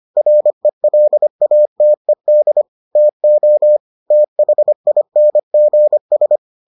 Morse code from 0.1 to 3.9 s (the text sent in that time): RELATED TO